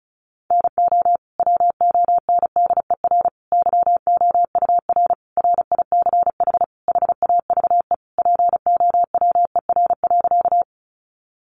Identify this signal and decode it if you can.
Morse code, heard as a keyed tone, about 34 wpm, 700 Hz. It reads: NO WONDER YOUR RICH HAVE POWER.
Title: Sentences from Top 1100 Words 35wpm